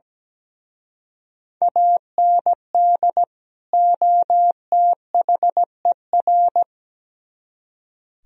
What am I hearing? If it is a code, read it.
Morse code, heard as a keyed tone, about 20 wpm, 700 Hz. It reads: AND OTHER
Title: Speed-Racing - Sets of 2 Words 17wpm